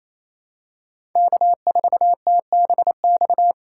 Morse code, heard as a keyed tone, about 27 wpm, 700 Hz. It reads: K4TBX
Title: USA Call Signs - 4x Character Spacing - 28wpm